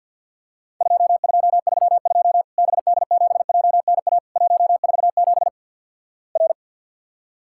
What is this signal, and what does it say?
Morse code, heard as a keyed tone, about 50 wpm, 700 Hz. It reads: 2232 BD7JNA 147 R
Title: Worked All Europe DX Contest - QTC Exchange with Repeats - 50wpm